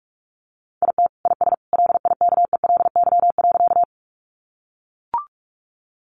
Morse code, heard as a keyed tone, about 44 wpm, 700 Hz. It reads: IT IS LIKELY.
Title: Sentences from Top 600 Words 45wpm